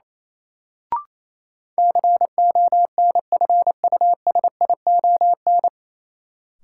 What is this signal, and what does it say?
Morse code, heard as a keyed tone, about 28 wpm, 700 Hz. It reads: CONFUSION
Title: Sentences from Top 1600 Words - New Word Drill - 28wpm